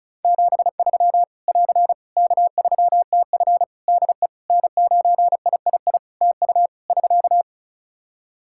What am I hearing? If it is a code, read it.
Morse code, heard as a keyed tone, about 36 wpm, 700 Hz. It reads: 73 <AR> K3TF DE N9III TU <SK>
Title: Practice QSO - Exchange 4 - Ending - 35wpm